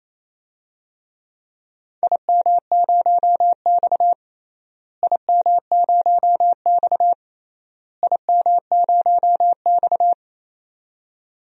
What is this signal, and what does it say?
Morse code, heard as a keyed tone, about 28 wpm, 700 Hz. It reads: IM0X IM0X IM0X
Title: DXCC-100 Call Signs - 4 Characters - Repeated 3x - 28wpm